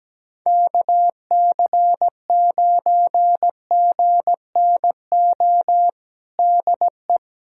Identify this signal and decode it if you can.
Morse code, heard as a keyed tone, about 17 wpm, 700 Hz. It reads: KC9GNO DE
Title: USA Callsign DE Callsign - 6 Characters - 17wpm